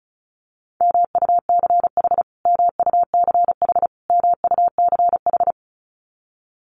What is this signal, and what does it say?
Morse code, heard as a keyed tone, about 34 wpm, 700 Hz. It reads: MUCH MUCH MUCH
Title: Top 200 Words Repeated 35wpm